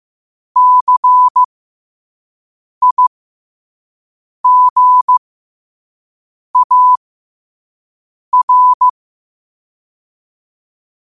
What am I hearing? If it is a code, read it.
Morse code, heard as a keyed tone, about 15 wpm, spaced as for 5 wpm, 1000 Hz. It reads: CIGAR
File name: qotdfiles / 5wpm